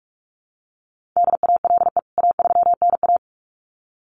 Morse code, heard as a keyed tone, about 44 wpm, 700 Hz. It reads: DALE A3NA
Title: CWT Exchange 45wpm